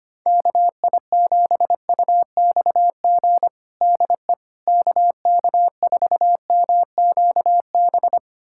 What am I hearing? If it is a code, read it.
Morse code, heard as a keyed tone, about 25 wpm, 700 Hz. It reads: KI7UXG DE KK4MQB
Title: USA Callsign DE Callsign - 6 Characters - 25wpm